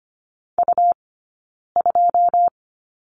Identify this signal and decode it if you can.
Morse code, heard as a keyed tone, about 25 wpm, 700 Hz. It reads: U 2
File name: Lesson 17d - Letter u - Two Characters 25wpm